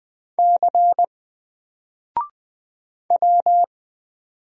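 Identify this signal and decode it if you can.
Morse code, heard as a keyed tone, about 20 wpm, 700 Hz. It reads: C W
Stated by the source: Single letter and number Morse drill at 20 wpm, from N4LCD.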